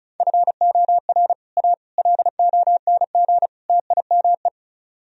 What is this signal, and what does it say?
Morse code, heard as a keyed tone, about 44 wpm, 700 Hz. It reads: FOR A LONG TIME
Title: Speed-Racing - Sets of 4 Words 35wpm